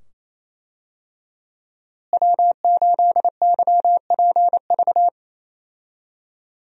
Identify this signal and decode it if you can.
Morse code, heard as a keyed tone, about 28 wpm, 700 Hz. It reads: W8YPV
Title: USA Call Signs - 5 Characters - Repeated 2x - 28wpm